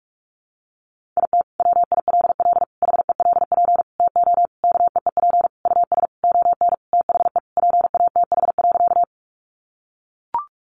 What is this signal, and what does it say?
Morse code, heard as a keyed tone, about 44 wpm, 700 Hz. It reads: IT WILL HELP TO KEEP US ON THE PATH.
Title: Sentences from Top 1400 Words 45wpm